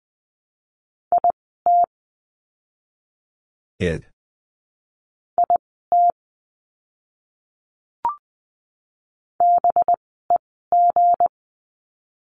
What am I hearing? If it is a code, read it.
Morse code, heard as a keyed tone, about 20 wpm, spaced as for 14 wpm, 700 Hz. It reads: IT IT BEG
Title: 185 Lesson 30f - Letter b - Words 20wpm